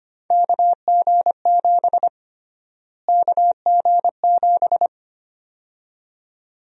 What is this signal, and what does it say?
Morse code, heard as a keyed tone, about 25 wpm, 700 Hz. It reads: KG7 KG7
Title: USA Call Sign Prefixes - Repeated 2x - 25wpm